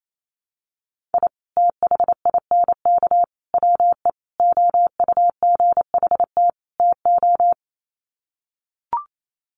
Morse code, heard as a keyed tone, about 28 wpm, 700 Hz. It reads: I THINK WE OUGHT TO
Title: Sets of 5 Words Encore 28wpm